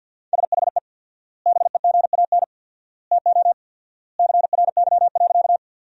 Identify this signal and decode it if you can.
Morse code, heard as a keyed tone, about 50 wpm, 700 Hz. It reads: SHE BEGAN TO CRY.
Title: 4x Word-Spacing - Sentences from Top 500 Words - 50wpm